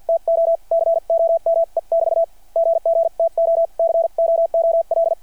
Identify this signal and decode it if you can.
Morse code, heard as a keyed tone, about 48 wpm, spaced as for 42 wpm, 660 Hz. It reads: TO COME<BT> GGTO COOL
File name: archived / CRSnet